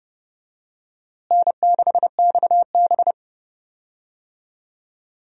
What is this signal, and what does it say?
Morse code, heard as a keyed tone, about 30 wpm, 700 Hz. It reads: N6XB
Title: USA Call Signs - 4 Characters - Repeated 2x - 30wpm